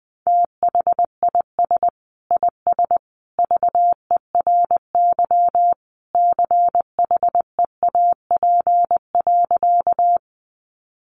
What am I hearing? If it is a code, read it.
Morse code, heard as a keyed tone, about 20 wpm, 700 Hz. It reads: THIS IS VERY CHEAP.
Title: Sentences from Top 1200 Words 20wpm